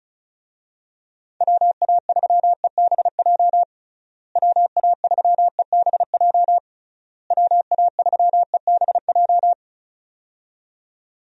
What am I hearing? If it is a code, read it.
Morse code, heard as a keyed tone, about 34 wpm, 700 Hz. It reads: WA3EBJ WA3EBJ WA3EBJ
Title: USA Call Signs - 6 Characters - Repeated 3x - 35wpm